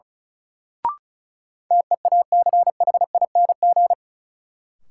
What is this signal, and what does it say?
Morse code, heard as a keyed tone, about 34 wpm, 700 Hz. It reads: TEACHING
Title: Sentences from Top 1100 Words - New Word Drill Alt - 35wpm